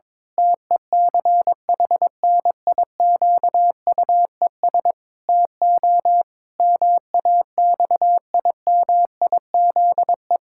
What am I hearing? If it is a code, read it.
Morse code, heard as a keyed tone, about 22 wpm, 700 Hz. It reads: TECHNIQUES TO MAXIMIZE
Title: Pangram Challenge v2 - 22wpm